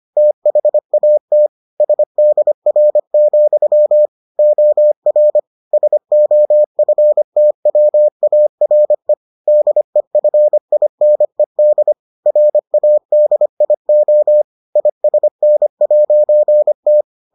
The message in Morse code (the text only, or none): THAT SDR, OR SOFTWARE DEFINED RADIO ISN'T